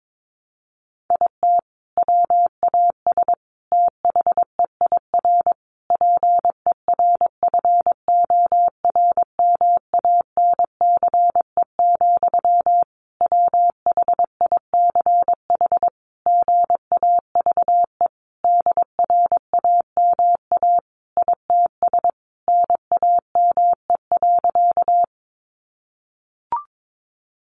Morse code, heard as a keyed tone, about 22 wpm, 700 Hz. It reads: IT WAS THEIR PERFORMANCE, WHICH GAVE DRAMA ITS NAME.
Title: Sentences from Top 1500 Words 22wpm